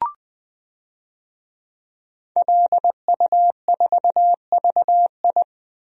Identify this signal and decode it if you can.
Morse code, heard as a keyed tone, about 20 wpm, 700 Hz. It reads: LU4VI
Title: DXCC-100 Call Signs - 5 Characters - Repeated 2x - 20wpm